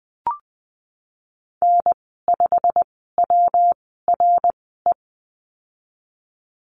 Morse code, heard as a keyed tone, about 19 wpm, spaced as for 14 wpm, 700 Hz. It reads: N5WRE
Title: 186 Lesson 30g - Letter b - Call Signs 20wpm